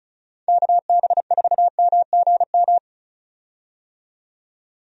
Morse code, heard as a keyed tone, about 36 wpm, 700 Hz. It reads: KB4MGM
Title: USA Call Signs - 6 Characters - 35wpm